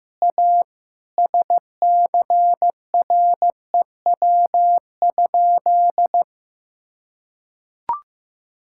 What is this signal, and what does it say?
Morse code, heard as a keyed tone, about 15 wpm, 700 Hz. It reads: A SCREW?
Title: Sentences from Top 1600 Words 15wpm